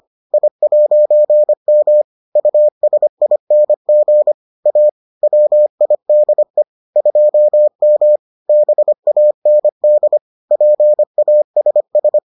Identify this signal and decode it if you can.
Morse code, heard as a keyed tone, about 25 wpm, 600 Hz. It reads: I'M USING A WIDE 2M BAND PASS